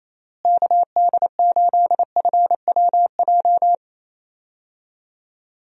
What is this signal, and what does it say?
Morse code, heard as a keyed tone, about 28 wpm, 700 Hz. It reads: KD8FWJ
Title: USA Call Signs - 6 Characters - Repeated 3x - 28wpm